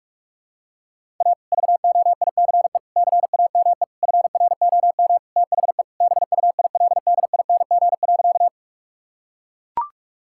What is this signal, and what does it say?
Morse code, heard as a keyed tone, about 42 wpm, 700 Hz. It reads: A VOICE CAME FROM THE BUILDING.
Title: Sentences from Top 1000 Words - Review Part 5 - 45wpm